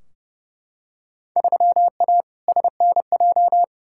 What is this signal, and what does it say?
Morse code, heard as a keyed tone, about 32 wpm, 700 Hz. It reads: 3A SNJ
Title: Field Day Exchanges - 30wpm